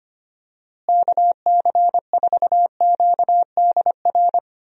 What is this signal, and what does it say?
Morse code, heard as a keyed tone, about 25 wpm, 700 Hz. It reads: KC4QDR
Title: USA Call Signs - 6 Characters - Repeated 2x - 25wpm